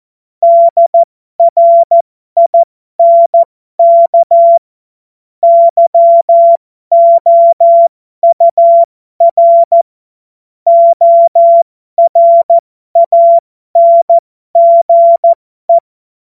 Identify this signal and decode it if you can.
Morse code, heard as a keyed tone, about 14 wpm, spaced as for 12 wpm, 680 Hz. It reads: DRINK YOUR ORANGE
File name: Morsekurs / Witze